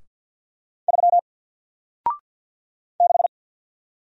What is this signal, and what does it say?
Morse code, heard as a keyed tone, about 48 wpm, 700 Hz. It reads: 3 6
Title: Single Letter-Number Alt 50wpm